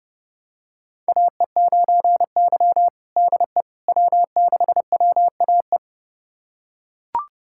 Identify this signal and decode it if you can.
Morse code, heard as a keyed tone, about 32 wpm, 700 Hz. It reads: AE9Y DE W6WAE
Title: USA Callsign DE Callsign 30wpm